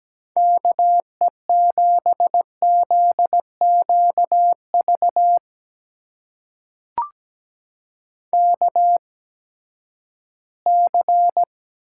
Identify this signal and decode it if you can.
Morse code, heard as a keyed tone, about 17 wpm, spaced as for 12 wpm, 700 Hz. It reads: KE7ZQV KC
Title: USA Call Signs - 6 Characters - 8x Character Spacing - 17wpm